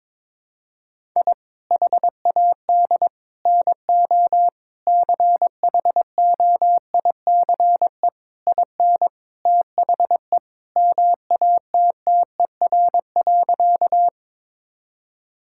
Morse code, heard as a keyed tone, about 22 wpm, 700 Hz. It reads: I HAD NO CHOICE IN THE MATTER.
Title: Sentences from Top 1000 Words - Review Part 6 - 22wpm